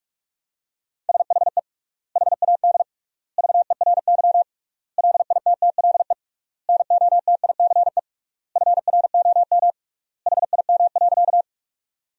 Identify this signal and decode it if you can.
Morse code, heard as a keyed tone, about 45 wpm, 700 Hz. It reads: SHE HAD VERY LITTLE NOTICE FROM HIM.
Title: 3x Word-Spacing - Sentences from Top 400 Words - 45wpm